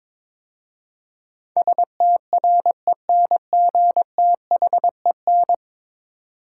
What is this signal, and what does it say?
Morse code, heard as a keyed tone, about 22 wpm, 700 Hz. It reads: STRENGTHEN
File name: practice / 22wpm / 10 Letter Words - First 3 Letters Given - 22wpm